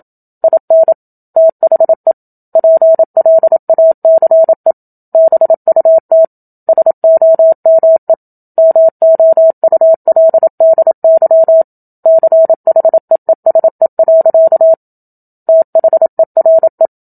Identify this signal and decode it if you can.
Morse code, heard as a keyed tone, about 27 wpm, 650 Hz. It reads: IN THE PLACE BUT SOME MOULDY CHEESE. THERE
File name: Book Two / 01-Chapter 1 - 27 WPM